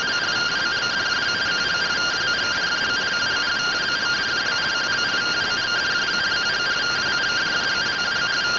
Two-tone teleprinter signal, about 170 Hz shift.